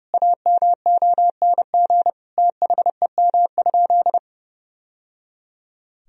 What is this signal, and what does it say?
Morse code, heard as a keyed tone, about 30 wpm, 700 Hz. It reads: AMONG THEM?
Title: Sentences from Top 1000 Words - Review Part 2 - 30wpm